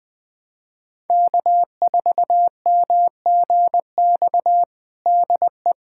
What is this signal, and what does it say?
Morse code, heard as a keyed tone, about 21 wpm, 700 Hz. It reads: K4MGX DE
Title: USA Callsign DE Callsign - 5 Characters - 20wpm